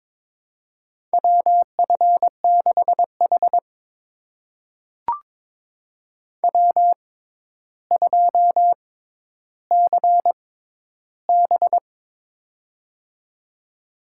Morse code, heard as a keyed tone, about 21 wpm, spaced as for 11 wpm, 700 Hz. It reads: WF6H W2CB
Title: USA Call Signs - 6x Character Spacing - 22wpm